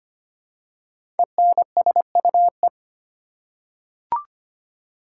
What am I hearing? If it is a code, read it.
Morse code, heard as a keyed tone, about 25 wpm, 700 Hz. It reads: ENSUE
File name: 5 Letter Words - First 3 Letters Given - 25wpm